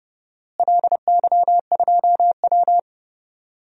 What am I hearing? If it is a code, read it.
Morse code, heard as a keyed tone, about 30 wpm, 700 Hz. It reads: LY2W
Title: DXCC-100 Call Signs - 4 Characters - Repeated 2x - 30wpm